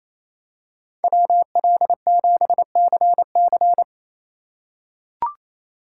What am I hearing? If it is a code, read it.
Morse code, heard as a keyed tone, about 28 wpm, 700 Hz. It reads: WL7CC
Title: USA Call Signs - 5 Characters - 2x Character Spacing - 28wpm